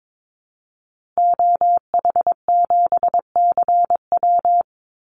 Morse code, heard as a keyed tone, about 22 wpm, 700 Hz. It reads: OH7CW
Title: Call Signs Repeated 2x - 22wpm